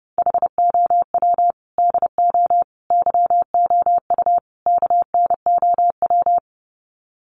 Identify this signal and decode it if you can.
Morse code, heard as a keyed tone, about 30 wpm, 700 Hz. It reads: HOW DO YOU KNOW
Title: Sets of 4 Words 30wpm